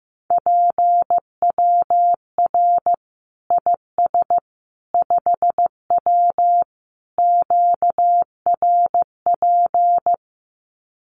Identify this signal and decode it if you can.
Morse code, heard as a keyed tone, about 15 wpm, 700 Hz. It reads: PWR IS 5W QRP
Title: Common QSO Phrases 15wpm